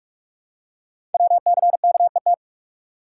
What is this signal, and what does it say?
Morse code, heard as a keyed tone, about 44 wpm, 700 Hz. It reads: WCKET
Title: Code Groups - 5 Letters - 45wpm